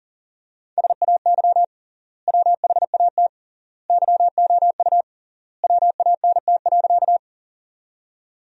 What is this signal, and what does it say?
Morse code, heard as a keyed tone, about 40 wpm, 700 Hz. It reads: SAY WHAT YOU WANT.
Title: 3x Word-Spacing - Sentences from Top 200 Words - 40wpm